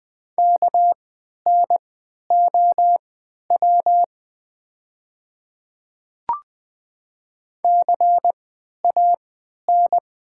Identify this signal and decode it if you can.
Morse code, heard as a keyed tone, about 20 wpm, spaced as for 11 wpm, 700 Hz. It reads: KNOW CAN
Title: 3x Character Spacing - Top 100 Words - 20wpm